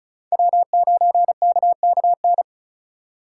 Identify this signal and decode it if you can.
Morse code, heard as a keyed tone, about 36 wpm, 700 Hz. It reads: W9KKN
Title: USA Call Signs - 5 Characters - 2x Character Spacing - 35wpm